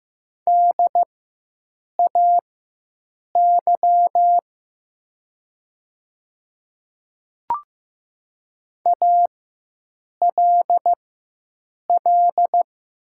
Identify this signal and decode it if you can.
Morse code, heard as a keyed tone, about 15 wpm, spaced as for 7 wpm, 700 Hz. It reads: DAY ALL
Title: 4x Character Spacing - Top 200 Words - 15wpm